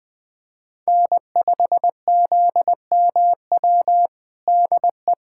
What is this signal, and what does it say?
Morse code, heard as a keyed tone, about 20 wpm, 700 Hz. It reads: N5ZMW DE
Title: USA Callsign DE Callsign - 5 Characters - 20wpm